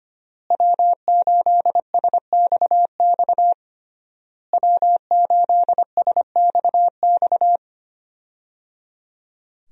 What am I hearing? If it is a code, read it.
Morse code, heard as a keyed tone, about 25 wpm, 700 Hz. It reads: W8SXX W8SXX
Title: USA Call Signs - 5 Characters - Repeated 3x - 25wpm